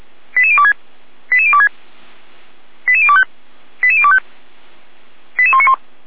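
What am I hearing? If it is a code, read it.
Five-tone selective call (ZVEI), heard as quick runs of five tones, tones of about 70 ms. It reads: 88026 88026 88025 88025 88191
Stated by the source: ZVEI five-tone call 89020 received with an SDR.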